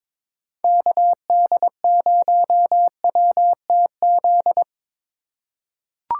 Morse code, heard as a keyed tone, about 22 wpm, 700 Hz. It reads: KD0WTZ
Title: USA Call Signs - 3x Character Spacing - 22wpm